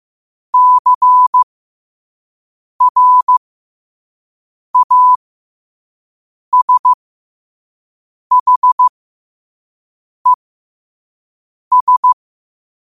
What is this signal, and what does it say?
Morse code, heard as a keyed tone, about 15 wpm, spaced as for 5 wpm, 1000 Hz. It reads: CRASHES
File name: qotdfiles / 5wpm